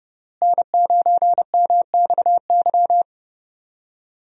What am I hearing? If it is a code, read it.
Morse code, heard as a keyed tone, about 31 wpm, 700 Hz. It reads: N9MXY
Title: USA Call Signs - 5 Characters - Repeated 3x - 30wpm